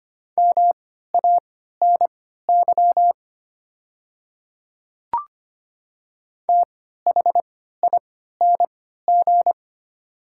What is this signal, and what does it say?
Morse code, heard as a keyed tone, about 25 wpm, spaced as for 14 wpm, 700 Hz. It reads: MANY THING
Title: 3x Character Spacing - Top 200 Words - 25wpm